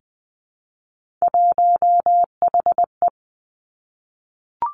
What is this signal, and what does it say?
Morse code, heard as a keyed tone, about 20 wpm, 700 Hz. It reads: 1HE
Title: Three Challenging Letter-Number Combinations 20wpm